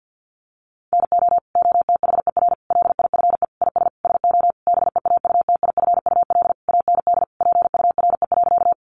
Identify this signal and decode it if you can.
Morse code, heard as a keyed tone, about 50 wpm, 700 Hz. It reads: NO OTHER LIFE IS SO BEAUTIFUL AND PURE.
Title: Sentences from Top 1500 Words 50wpm